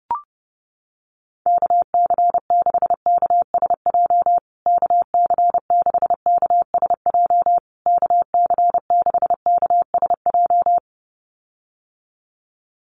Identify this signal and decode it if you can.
Morse code, heard as a keyed tone, about 30 wpm, 700 Hz. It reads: KC6KSJ KC6KSJ KC6KSJ
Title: USA Call Signs Repeated 3x 30wpm